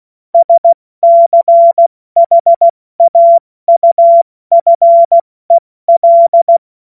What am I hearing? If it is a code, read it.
Morse code, heard as a keyed tone, about 16 wpm, 680 Hz. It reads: SCHAUFEL
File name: Morsekurs / Witze / Witz 28.November 2023 70 F80